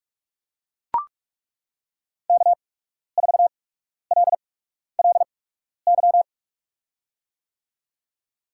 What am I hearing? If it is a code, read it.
Morse code, heard as a keyed tone, about 41 wpm, spaced as for 12 wpm, 700 Hz. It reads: K4LLY
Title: USA Call Signs - 5 Characters - 8x Character Spacing - 45wpm